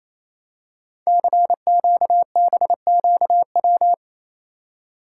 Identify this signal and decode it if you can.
Morse code, heard as a keyed tone, about 27 wpm, 700 Hz. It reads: CQBQW
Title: Code Groups - 5 Letter:Numbers - 28wpm